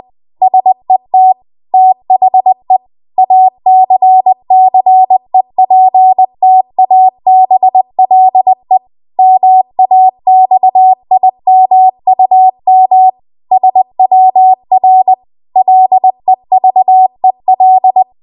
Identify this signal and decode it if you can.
Morse code, heard as a keyed tone, about 20 wpm, 750 Hz. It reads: SET THE ACCEPTABLE MAXIMUM SWR LEVEL
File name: Archive / 20 WPM / 20WPM